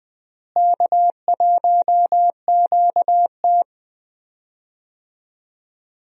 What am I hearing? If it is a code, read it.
Morse code, heard as a keyed tone, about 20 wpm, 700 Hz. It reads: K1QT
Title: USA Call Signs - 4 Characters - Repeated 2x - 20wpm